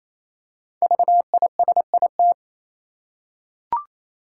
Morse code, heard as a keyed tone, about 28 wpm, 700 Hz. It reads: VISIT
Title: Five Letter Words - 28wpm